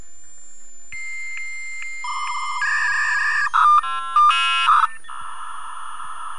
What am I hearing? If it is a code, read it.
Dial-up modem handshake.